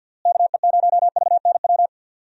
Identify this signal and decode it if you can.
Morse code, heard as a keyed tone, about 47 wpm, 700 Hz. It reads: KE0VNW
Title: USA Call Signs - 6 Characters - Repeated 3x - 50wpm